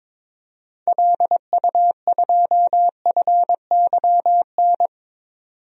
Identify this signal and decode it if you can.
Morse code, heard as a keyed tone, about 22 wpm, 700 Hz. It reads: LU2FYN
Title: DXCC-100 Call Signs - 6 Characters - 22wpm